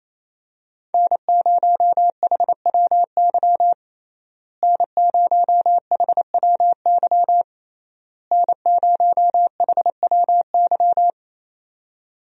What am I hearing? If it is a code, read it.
Morse code, heard as a keyed tone, about 27 wpm, 700 Hz. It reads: N0HWY N0HWY N0HWY
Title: USA Call Signs - 5 Characters - Repeated 3x - 28wpm